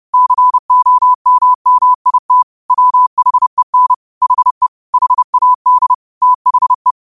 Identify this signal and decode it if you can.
Morse code, heard as a keyed tone, about 30 wpm, 1000 Hz. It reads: COMMIT WHEN HE HAD THE